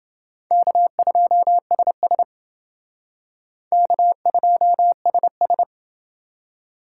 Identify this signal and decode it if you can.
Morse code, heard as a keyed tone, about 28 wpm, 700 Hz. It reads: K2SS K2SS